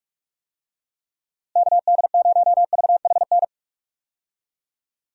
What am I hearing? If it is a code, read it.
Morse code, heard as a keyed tone, about 47 wpm, 700 Hz. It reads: KD0VHN
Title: USA Call Signs - 6 Characters - Repeated 2x - 45wpm